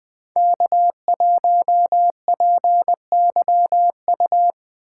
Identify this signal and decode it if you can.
Morse code, heard as a keyed tone, about 23 wpm, 700 Hz. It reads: K1PYU
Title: Speed Racing - USA 1x3 Call Signs 20wpm